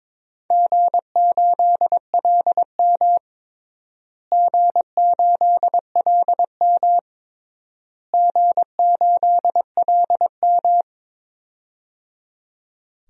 Morse code, heard as a keyed tone, about 22 wpm, 700 Hz. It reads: G8LM G8LM G8LM
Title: DXCC-100 Call Signs - 4 Characters - Repeated 3x - 22wpm